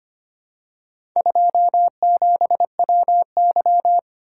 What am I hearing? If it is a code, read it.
Morse code, heard as a keyed tone, about 26 wpm, 700 Hz. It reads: 27WY
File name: practice / 25wpm / Four Letter-Number Combinations - 25wpm